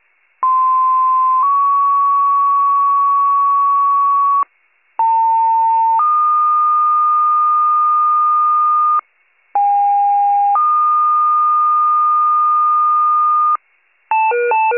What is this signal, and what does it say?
Two-tone page, heard as a one-second tone, then a three-second tone, about 1040 then 1120 Hz, about 900 then 1220 Hz, about 810 then 1220 Hz. It is heard from 0.4 to 13.6 s.